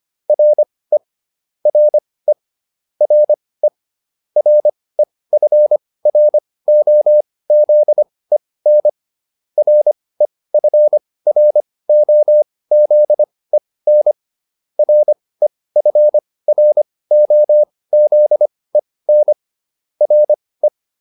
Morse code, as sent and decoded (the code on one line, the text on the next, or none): .-. / .-. / .-. / .-. ..-. .-. --- --.. -. / .-. ..-. .-. --- --.. -. / .-. ..-. .-. --- --.. -. / .-. .
RE RE RE REFROZEN REFROZEN REFROZEN RE